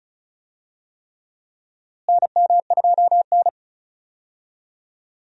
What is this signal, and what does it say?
Morse code, heard as a keyed tone, about 36 wpm, 700 Hz. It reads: NM2N
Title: USA Call Signs - 4 Characters - Repeated 3x - 35wpm